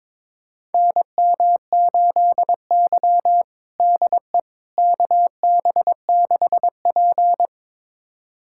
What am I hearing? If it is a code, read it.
Morse code, heard as a keyed tone, about 22 wpm, 700 Hz. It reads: NM8Y DE KB6P
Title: USA Callsign DE Callsign - 4 Characters - 22wpm